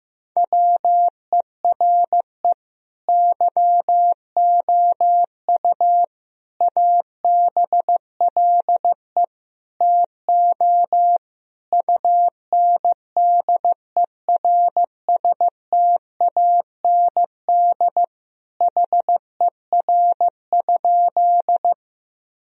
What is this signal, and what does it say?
Morse code, heard as a keyed tone, about 15 wpm, 700 Hz. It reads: WERE YOU ABLE TO UNDERSTAND HER?
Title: Sentences from Top 1000 Words - Review Part 2 - 15wpm